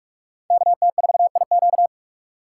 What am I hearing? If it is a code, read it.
Morse code, heard as a keyed tone, about 45 wpm, 700 Hz. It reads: KT4IQ